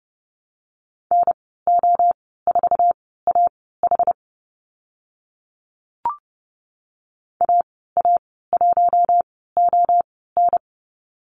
Morse code, heard as a keyed tone, about 30 wpm, spaced as for 17 wpm, 700 Hz. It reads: NO4AH AA1ON
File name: Lesson 14g - Number 2 - Call Signs 30wpm